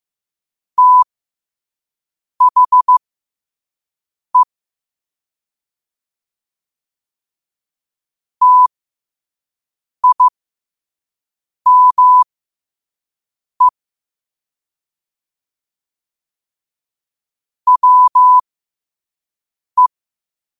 Morse code, heard as a keyed tone, about 15 wpm, spaced as for 5 wpm, 1000 Hz. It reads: THE TIME WE
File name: qotdfiles / 5wpm